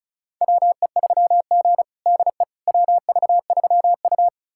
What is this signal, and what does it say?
Morse code, heard as a keyed tone, about 36 wpm, 700 Hz. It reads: WE3G DE WV3U
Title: USA Callsign DE Callsign - 4 Characters - 35wpm